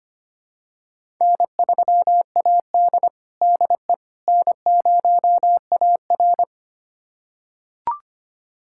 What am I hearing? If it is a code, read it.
Morse code, heard as a keyed tone, about 25 wpm, 700 Hz. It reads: N3AD DE N0AR
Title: USA Callsign DE Callsign - 4 Characters - 25wpm